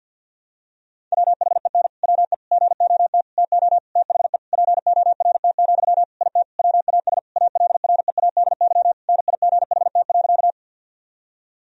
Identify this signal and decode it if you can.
Morse code, heard as a keyed tone, about 48 wpm, 700 Hz. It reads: WHEN WE GOT TO THE PORT, IT WAS ALREADY NIGHT.
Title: Sentences from Top 1000 Words - Review Part 1 - 50wpm